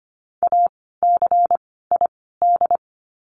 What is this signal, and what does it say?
Morse code, heard as a keyed tone, about 25 wpm, spaced as for 16 wpm, 700 Hz. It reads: ACID
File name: course / 25:10wpm / 094 Lesson 16f - Letter c - Words 25wpm